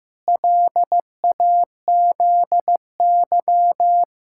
Morse code, heard as a keyed tone, about 16 wpm, 700 Hz. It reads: LAZY
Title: Pangram Challenge v2 - 15wpm